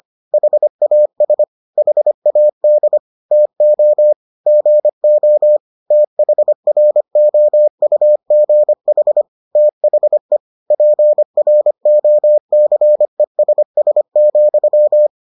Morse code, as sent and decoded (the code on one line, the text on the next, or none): .... .- ... / .... .- -.. / - --- / --. --- / - .... .-. --- ..- --. .... / - .... / .--. .-. --- -.-. ... ... --..--
HAS HAD TO GO THROUGH THE PROCESS,